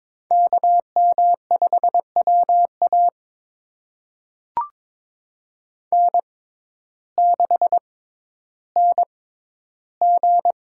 Morse code, heard as a keyed tone, about 22 wpm, spaced as for 12 wpm, 700 Hz. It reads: KM5WA N6NG
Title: USA Call Signs - 5 Characters - 6x Character Spacing - 22wpm